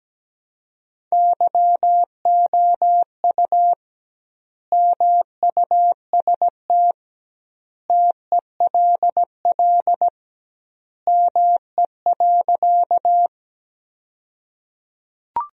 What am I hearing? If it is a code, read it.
Morse code, heard as a keyed tone, about 17 wpm, 700 Hz. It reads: YOU MUST TELL ME.
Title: 2x Word-Spacing - Sentences from Top 200 Words - 17wpm